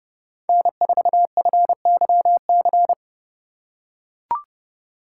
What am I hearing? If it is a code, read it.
Morse code, heard as a keyed tone, about 30 wpm, 700 Hz. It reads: N4FYC